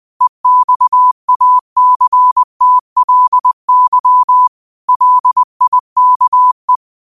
Morse code, heard as a keyed tone, about 20 wpm, 1000 Hz. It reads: EXACTLY LIKE
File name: qotdfiles / 20wpm